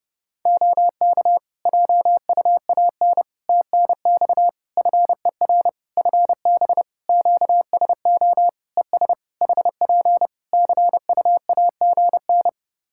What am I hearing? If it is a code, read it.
Morse code, heard as a keyed tone, about 30 wpm, 700 Hz. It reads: OK JUAN TNX FER FB QSO ES HP CUAGN
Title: Practice QSO - Exchange 4 - Ending - 30wpm